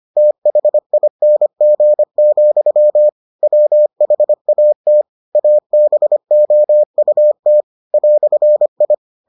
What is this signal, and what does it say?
Morse code, heard as a keyed tone, about 25 wpm, 600 Hz. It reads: THING, WHAT ABOUT "I